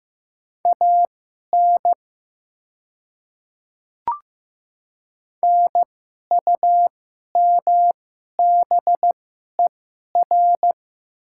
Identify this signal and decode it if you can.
Morse code, heard as a keyed tone, about 15 wpm, spaced as for 11 wpm, 700 Hz. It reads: AN NUMBER
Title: 2x Character Spacing - Top 100 Words - 15wpm